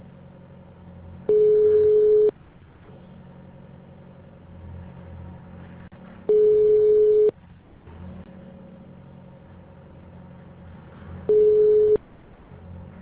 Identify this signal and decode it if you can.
Ringback tone (Europe, 425 Hz).